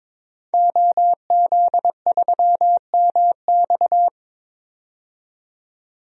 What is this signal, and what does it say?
Morse code, heard as a keyed tone, about 22 wpm, 700 Hz. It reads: OZ3MX
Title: DXCC-100 Call Signs - 5 Characters - 22wpm